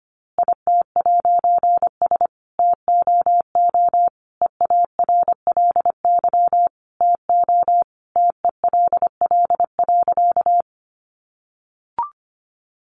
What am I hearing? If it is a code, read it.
Morse code, heard as a keyed tone, about 25 wpm, 700 Hz. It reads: IT'S TOO EARLY TO TELL.
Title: A Sound Of Thunder 25wpm